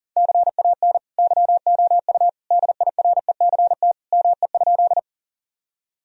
Morse code, heard as a keyed tone, about 40 wpm, 700 Hz. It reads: CAN YOU DIRECT ME?
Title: Sentences from Top 1000 Words - Review Part 2 - 40wpm